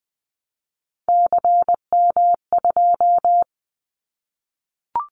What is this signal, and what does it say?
Morse code, heard as a keyed tone, about 20 wpm, 700 Hz. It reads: CM2
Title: Callsign Prefixes for Top 100 DXCC - 20wpm